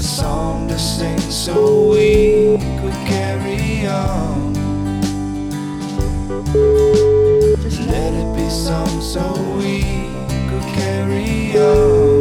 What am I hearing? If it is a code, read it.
Ringback tone (Europe, 425 Hz), with noise.